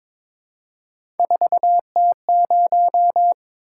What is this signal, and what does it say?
Morse code, heard as a keyed tone, about 22 wpm, 700 Hz. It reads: 4T0
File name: practice / 22wpm / Three Letter-Number Combinations - 22wpm